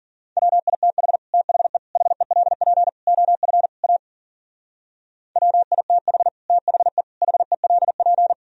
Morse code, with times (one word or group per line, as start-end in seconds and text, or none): 0.4-1.2 WITH
1.3-1.8 THE
2.0-2.9 HELP
3.1-3.7 OF
3.8-4.0 A
5.4-6.3 WITH
6.5-7.0 THE
7.2-8.3 HELP